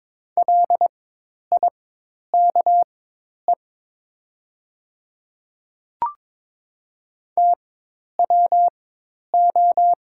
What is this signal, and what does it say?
Morse code, heard as a keyed tone, about 22 wpm, spaced as for 10 wpm, 700 Hz. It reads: LIKE TWO
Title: 4x Character Spacing - Top 100 Words - 22wpm